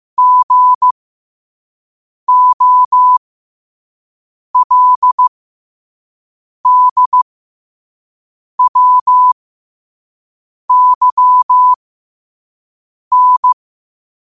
Morse code, heard as a keyed tone, about 15 wpm, spaced as for 5 wpm, 1000 Hz. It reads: GOLDWYN
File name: qotdfiles / 5wpm